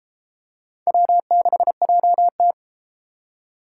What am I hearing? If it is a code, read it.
Morse code, heard as a keyed tone, about 33 wpm, 700 Hz. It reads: W6JT